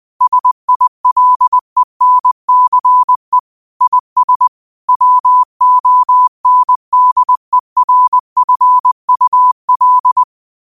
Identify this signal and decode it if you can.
Morse code, heard as a keyed tone, about 20 wpm, 1000 Hz. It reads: SILENCE IS WONDERFUL
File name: 20wpm